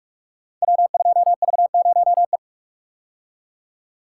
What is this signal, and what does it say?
Morse code, heard as a keyed tone, about 44 wpm, 700 Hz. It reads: W2V0E
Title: Code Groups - 5 Letter:Numbers - 45wpm